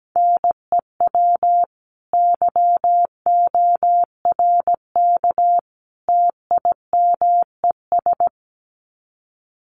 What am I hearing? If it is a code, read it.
Morse code, heard as a keyed tone, about 17 wpm, 700 Hz. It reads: NEW YORK TIMES
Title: Sets of 3 Words 17wpm